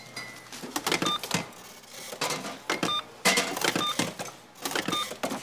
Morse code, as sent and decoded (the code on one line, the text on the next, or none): none